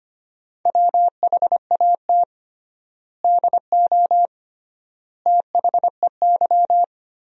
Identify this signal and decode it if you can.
Morse code, heard as a keyed tone, about 25 wpm, 700 Hz. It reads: WHAT DO THEY
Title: 3x Word-Spacing - Sentences from Top 200 Words - 25wpm